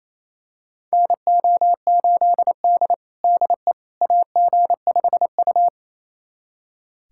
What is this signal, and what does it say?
Morse code, heard as a keyed tone, about 28 wpm, 700 Hz. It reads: NO8D DE AG5U
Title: USA Callsign DE Callsign - 4 Characters - 28wpm